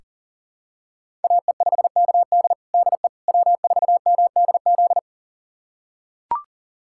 Morse code, heard as a keyed tone, about 37 wpm, 700 Hz. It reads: AE5KD DE W4MDZ
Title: USA Callsign DE Callsign - 5 Characters - 40wpm